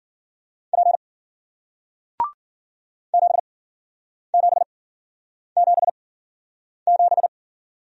Morse code, heard as a keyed tone, about 50 wpm, spaced as for 9 wpm, 700 Hz. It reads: P 7777